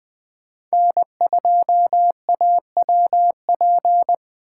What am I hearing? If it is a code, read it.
Morse code, heard as a keyed tone, about 20 wpm, 700 Hz. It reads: N2AWP
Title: USA Call Signs - 8x Character Spacing - 20wpm